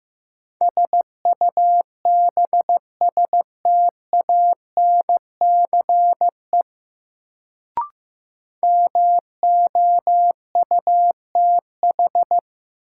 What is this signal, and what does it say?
Morse code, heard as a keyed tone, about 15 wpm, 700 Hz. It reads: SUBSTANCE MOUTH
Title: Sentences from Top 1500 Words - New Word Drill Alt - 15wpm